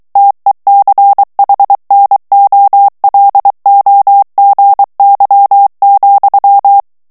Morse code, heard as a keyed tone, about 23 wpm, 800 Hz. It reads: TECHNOLOGY,
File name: ANGLAIS texte clair 3 vitesse 22 mots minute